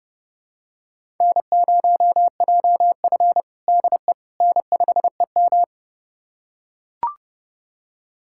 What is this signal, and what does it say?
Morse code, heard as a keyed tone, about 30 wpm, 700 Hz. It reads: N0JF DE N5EM
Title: USA Callsign DE Callsign - 4 Characters - 30wpm